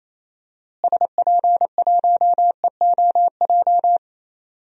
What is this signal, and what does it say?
Morse code, heard as a keyed tone, about 28 wpm, 700 Hz. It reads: SP1EOJ
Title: DXCC-100 Call Signs - 6 Characters - Repeated 3x - 28wpm